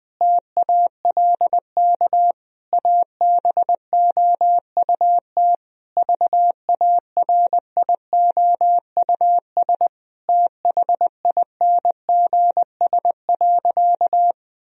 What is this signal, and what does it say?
Morse code, heard as a keyed tone, about 20 wpm, 700 Hz. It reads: TALK ABOUT VARIOUS THINGS.
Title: Sentences from Top 1000 Words - Review Part 4 - 20wpm